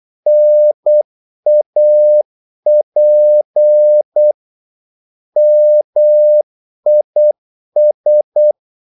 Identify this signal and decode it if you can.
Morse code, heard as a keyed tone, about 8 wpm, 600 Hz. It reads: NAP MIS
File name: Korte ord DK 8wpm